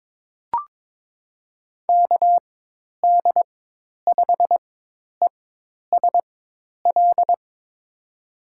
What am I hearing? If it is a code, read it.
Morse code, heard as a keyed tone, about 21 wpm, spaced as for 10 wpm, 700 Hz. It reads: KD5ESL
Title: USA Call Signs - 6 Characters - 4x Character Spacing - 22wpm